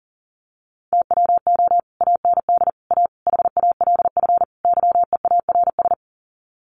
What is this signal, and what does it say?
Morse code, heard as a keyed tone, about 40 wpm, 700 Hz. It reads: TWO AND A HALF YEARS
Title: Sets of 5 Words 40wpm